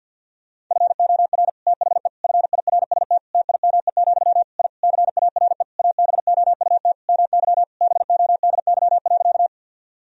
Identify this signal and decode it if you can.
Morse code, heard as a keyed tone, about 52 wpm, 700 Hz. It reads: FOR THE FIRST TIME, I CARE ABOUT MY BODY.
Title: Sentences from Top 1000 Words - Review Part 2 - 50wpm